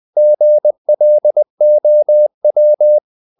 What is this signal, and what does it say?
Morse code, heard as a keyed tone, about 20 wpm, 600 Hz. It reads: GLOW
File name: the old man and the sea part 2 20wpm